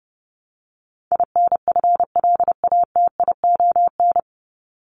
Morse code, heard as a keyed tone, about 30 wpm, 700 Hz. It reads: INFLATION
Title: Top 1001-1250 Words 30wpm